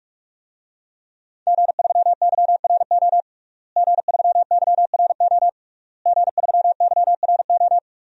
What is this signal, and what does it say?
Morse code, heard as a keyed tone, about 46 wpm, 700 Hz. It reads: G3YRO G3YRO G3YRO
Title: DXCC-100 Call Signs - 5 Characters - Repeated 3x - 45wpm